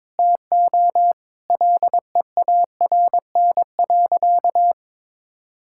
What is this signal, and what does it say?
Morse code, heard as a keyed tone, about 22 wpm, 700 Hz. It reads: TO LEARN.
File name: practice / 22wpm / Sentences with 2-Word Phrases - 22wpm